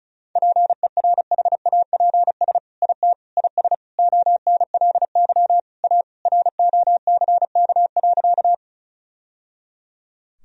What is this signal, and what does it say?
Morse code, heard as a keyed tone, about 36 wpm, 700 Hz. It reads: PERHAPS IT IS ONLY A ROCK.
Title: Sentences from Top 1000 Words - Review Part 2 - 35wpm